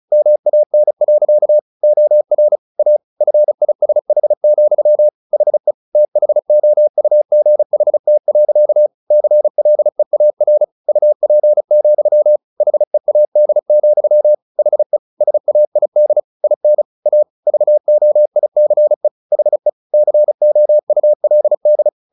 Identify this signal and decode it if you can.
Morse code, heard as a keyed tone, about 34 wpm, 600 Hz. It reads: MAN. OR A FISH, HE THOUGHT. CLEAR UP, HEAD, HE SAID IN A VOICE HE COULD
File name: the old man and the sea 2 35wpm